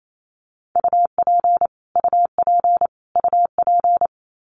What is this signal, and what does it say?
Morse code, heard as a keyed tone, about 28 wpm, 700 Hz. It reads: UP UP UP
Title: Top 500 Words Repeated 28wpm